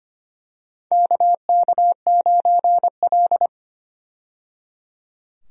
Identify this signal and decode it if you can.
Morse code, heard as a keyed tone, about 25 wpm, 700 Hz. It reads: KK9L
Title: USA Call Signs - 4 Characters - 25wpm